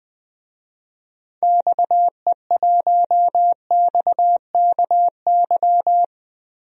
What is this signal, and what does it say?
Morse code, heard as a keyed tone, about 20 wpm, 700 Hz. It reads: XE1XKY
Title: DXCC-100 Call Signs - 6 Characters - Repeated 3x - 20wpm